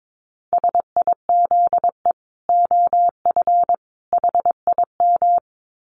Morse code, heard as a keyed tone, about 22 wpm, 700 Hz. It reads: SIZE OF HIM
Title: Dit Challenge 22wpm